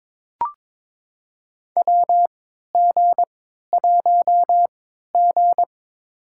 Morse code, heard as a keyed tone, about 21 wpm, spaced as for 12 wpm, 700 Hz. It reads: WG1G
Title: USA Call Signs - 3x Character Spacing - 22wpm